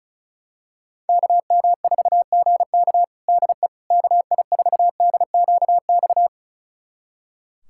Morse code, heard as a keyed tone, about 36 wpm, 700 Hz. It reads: KM4GK DE KI4DQX
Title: USA Callsign DE Callsign 35wpm